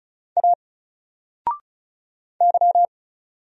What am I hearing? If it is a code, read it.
Morse code, heard as a keyed tone, about 35 wpm, 700 Hz. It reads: A Y
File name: Single-Letter-Number-35wpm